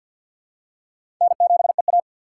Morse code, heard as a keyed tone, about 46 wpm, 700 Hz. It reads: N7EA
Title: USA Call Signs - 4 Characters - 6x Character Spacing - 50wpm